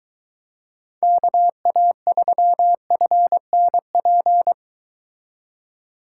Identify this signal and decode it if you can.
Morse code, heard as a keyed tone, about 23 wpm, 700 Hz. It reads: KA3FNP